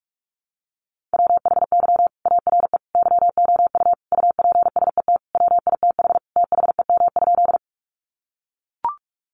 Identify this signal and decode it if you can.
Morse code, heard as a keyed tone, about 44 wpm, 700 Hz. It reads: WHY ARE YOU UPSET WITH THEM?
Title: Sentences from Top 800 Words 45wpm